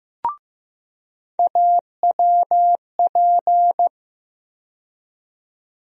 Morse code, heard as a keyed tone, about 15 wpm, 700 Hz. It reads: AWP
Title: USA Call Sign Postfixes - 3 Letters 15wpm